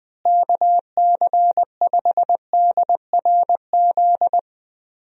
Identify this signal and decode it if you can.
Morse code, heard as a keyed tone, about 20 wpm, 700 Hz. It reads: KC5DRZ
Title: USA Call Signs - 6 Characters - Repeated 2x - 20wpm